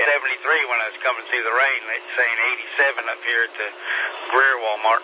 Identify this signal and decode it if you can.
Emergency Alert System (SAME) tones, heard as harsh buzzing bursts, with speech. It is heard from 0.2 to 4.1 s.